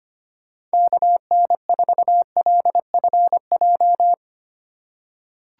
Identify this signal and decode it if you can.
Morse code, heard as a keyed tone, about 25 wpm, 700 Hz. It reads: KN4LFJ
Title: USA Call Signs - 6 Characters - 2x Character Spacing - 25wpm